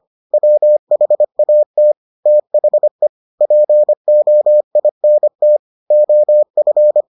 Morse code, heard as a keyed tone, about 25 wpm, 600 Hz. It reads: WHAT THE POINT OF